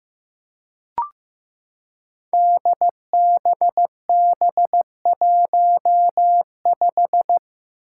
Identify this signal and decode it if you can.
Morse code, heard as a keyed tone, about 15 wpm, 700 Hz. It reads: DBB15
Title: Five Challenging Letter-Number Combinations 15wpm